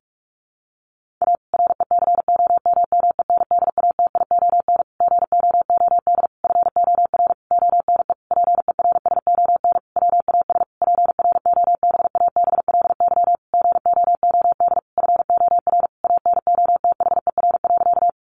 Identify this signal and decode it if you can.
Morse code, heard as a keyed tone, about 44 wpm, 700 Hz. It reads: A RECOMMENDATION GOOD FOR ONE PERSON WAS PROBABLY GOOD FOR ANOTHER.
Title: Sentences from Top 1400 Words 45wpm